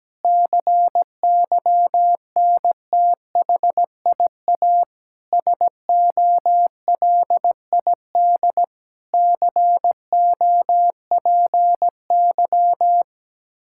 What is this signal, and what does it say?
Morse code, heard as a keyed tone, about 17 wpm, 700 Hz. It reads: CYNTHIA SOLID COPY
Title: K1USN SST QSO Practice 17wpm